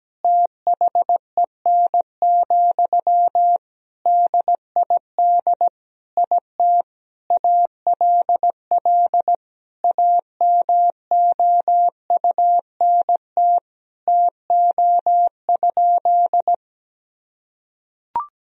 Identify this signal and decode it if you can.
Morse code, heard as a keyed tone, about 17 wpm, 700 Hz. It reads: THEN, DID IT ALL AMOUNT TO?
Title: Sentences from Top 1000 Words - Review Part 4 - 17wpm